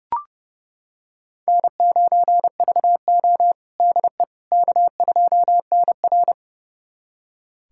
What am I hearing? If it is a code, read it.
Morse code, heard as a keyed tone, about 30 wpm, 700 Hz. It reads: N9VO DE K2NR
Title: USA Callsign DE Callsign - 4 Characters - 30wpm